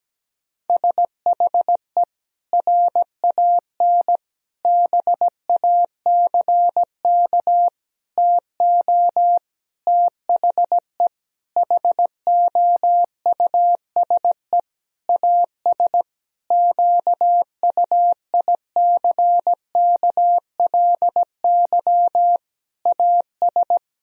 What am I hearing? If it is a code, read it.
Morse code, heard as a keyed tone, about 17 wpm, 700 Hz. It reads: SHE RAN BACK TO THE HOUSE AS QUICKLY AS
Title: Sentences from Top 1000 Words - Review Part 3 - 17wpm